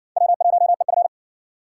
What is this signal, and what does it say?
Morse code, heard as a keyed tone, about 53 wpm, 700 Hz. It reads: W1EF